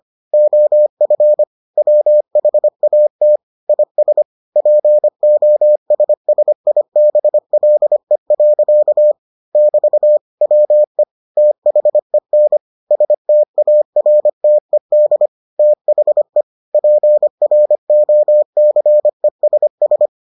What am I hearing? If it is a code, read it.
Morse code, heard as a keyed tone, about 25 wpm, 600 Hz. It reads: OF WHAT IS POSSIBLE. <BT> WE THEN STARTED THE PROCESS